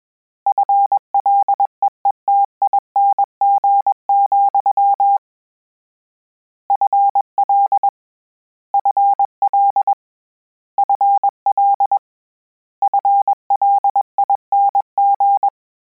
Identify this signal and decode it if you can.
Morse code, heard as a keyed tone, about 21 wpm, 800 Hz. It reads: FLEETING, FL FL FL FLING
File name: Combinations / Combinations-mp3s / Combinations@20